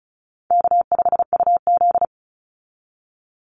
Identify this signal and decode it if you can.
Morse code, heard as a keyed tone, about 34 wpm, 700 Hz. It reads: K5UZ
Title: Easy Call Signs Repeated 3x - 35wpm